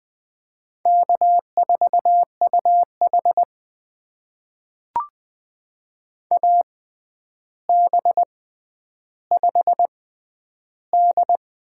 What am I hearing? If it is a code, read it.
Morse code, heard as a keyed tone, about 20 wpm, spaced as for 10 wpm, 700 Hz. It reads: K4UH AB5D
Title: USA Call Signs - 4 Characters - 6x Character Spacing - 20wpm